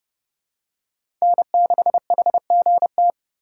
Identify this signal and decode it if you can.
Morse code, heard as a keyed tone, about 32 wpm, 700 Hz. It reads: N6HGT
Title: USA Call Signs - 5 Characters - 2x Character Spacing - 30wpm